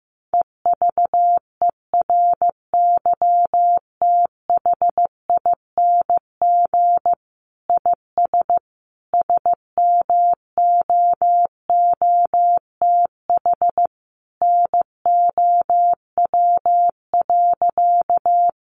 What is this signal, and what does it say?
Morse code, heard as a keyed tone, about 15 wpm, 700 Hz. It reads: EVERYTHING IS SMOOTH NOW.
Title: Sentences from Top 1400 Words 15wpm